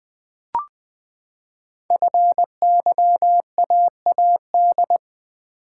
Morse code, heard as a keyed tone, about 20 wpm, 700 Hz. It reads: FYAAD
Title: Code Groups - 5 Letters - 20wpm